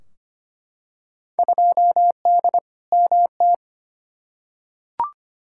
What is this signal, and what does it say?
Morse code, heard as a keyed tone, about 25 wpm, 700 Hz. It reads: 2D MT
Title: Field Day Exchanges Encore - 25wpm